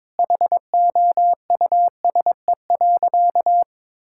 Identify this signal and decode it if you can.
Morse code, heard as a keyed tone, about 22 wpm, 700 Hz. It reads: HOUSE.
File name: practice / 22wpm / Sentences with 2-Word Phrases - 22wpm